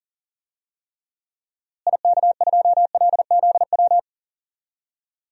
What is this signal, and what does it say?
Morse code, heard as a keyed tone, about 40 wpm, 700 Hz. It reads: IK2LZW
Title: DXCC-100 Call Signs - 6 Characters - Repeated 2x - 40wpm